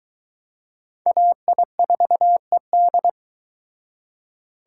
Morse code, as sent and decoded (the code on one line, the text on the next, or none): .- .. ....- . -..
AI4ED